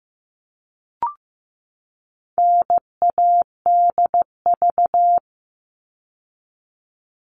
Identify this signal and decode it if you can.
Morse code, heard as a keyed tone, about 15 wpm, 700 Hz. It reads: NADV
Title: All Cut Numbers - 4-Digit 15wpm